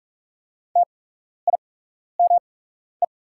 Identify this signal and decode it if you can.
Morse code, heard as a keyed tone, about 44 wpm, spaced as for 12 wpm, 690 Hz. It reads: TIME
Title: 8x Character Spacing - Top 100 Words - 45wpm